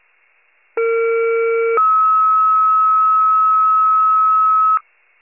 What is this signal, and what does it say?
Two-tone page, heard as a one-second tone, then a three-second tone, about 460 then 1290 Hz. It is heard from 0.8 to 4.8 s.
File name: wcalert